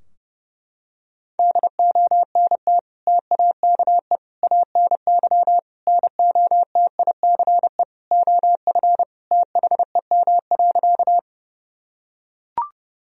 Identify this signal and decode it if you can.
Morse code, heard as a keyed tone, about 32 wpm, 700 Hz. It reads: DONT TAKE ANY NOTICE OF THEM.
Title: Sentences from Top 1000 Words - Review Part 2 - 30wpm